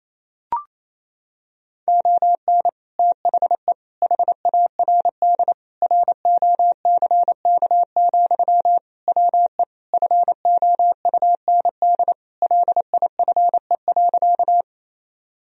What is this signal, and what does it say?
Morse code, heard as a keyed tone, about 28 wpm, 700 Hz. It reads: ON THE HARD ROCK, WE FOUND LIFE.
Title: Sentences from Top 1000 Words - Review Part 2 - 28wpm